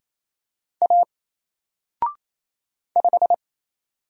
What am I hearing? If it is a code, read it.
Morse code, heard as a keyed tone, about 27 wpm, 700 Hz. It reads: A 5
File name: Single-Letter-Number-28wpm